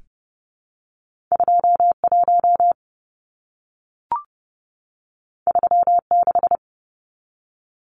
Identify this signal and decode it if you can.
Morse code, heard as a keyed tone, about 30 wpm, 700 Hz. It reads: 21 36
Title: Numbers - 2-Digit 30wpm